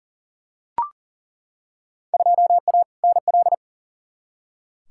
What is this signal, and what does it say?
Morse code, heard as a keyed tone, about 40 wpm, 700 Hz. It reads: 2A NL
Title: Field Day Exchanges Encore - 40wpm